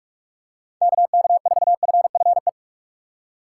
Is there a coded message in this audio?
Yes: Morse code from 0.8 to 2.5 s, heard as a keyed tone, about 46 wpm, 700 Hz. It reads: KK4FFE